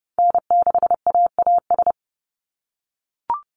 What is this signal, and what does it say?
Morse code, heard as a keyed tone, about 30 wpm, 700 Hz. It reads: N6AAS
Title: USA Call Signs 30wpm